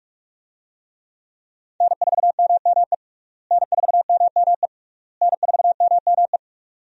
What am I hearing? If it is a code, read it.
Morse code, heard as a keyed tone, about 42 wpm, 700 Hz. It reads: N4MME N4MME N4MME
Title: USA Call Signs - 5 Characters - Repeated 3x - 45wpm